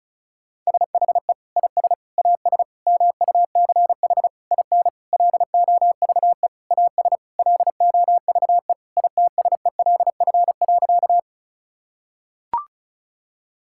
Morse code, heard as a keyed tone, about 36 wpm, 700 Hz. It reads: SHE IS AS MUCH IN LOVE AS LOVE ITSELF.
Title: Sentences from Top 1000 Words - Review Part 5 - 35wpm